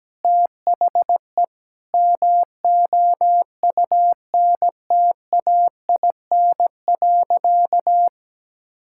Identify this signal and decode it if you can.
Morse code, heard as a keyed tone, about 17 wpm, 700 Hz. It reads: THE MOUNTAIN.
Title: Sentences from Top 1000 Words - Review Part 5 - 17wpm